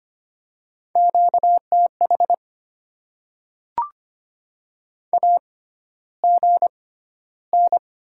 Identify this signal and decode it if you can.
Morse code, heard as a keyed tone, about 25 wpm, spaced as for 13 wpm, 700 Hz. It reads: QTH AGN
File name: Top 20 QSO Elements - 6x Character Spacing - 25wpm